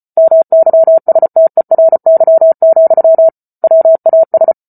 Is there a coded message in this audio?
Yes: Morse code from 0.2 to 4.5 s, heard as a keyed tone, about 34 wpm, 650 Hz. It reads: MYSTERY, WAS